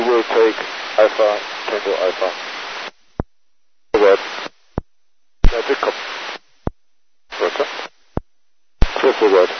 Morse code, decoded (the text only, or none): none